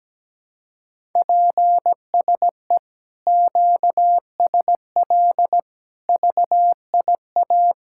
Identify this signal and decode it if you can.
Morse code, heard as a keyed tone, about 17 wpm, 700 Hz. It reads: PSE QSL VIA
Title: Speed-Racing Common QSO Phrases - 17wpm